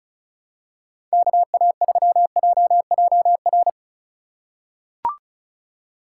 Morse code, heard as a keyed tone, about 36 wpm, 700 Hz. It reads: KA3JJR
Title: USA Call Signs - 6 Characters - 4x Character Spacing - 35wpm